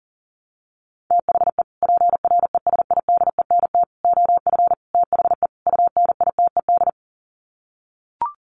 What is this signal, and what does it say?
Morse code, heard as a keyed tone, about 40 wpm, 700 Hz. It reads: THE PRESIDENT OF THE UNITED
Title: Sets of 5 Words 40wpm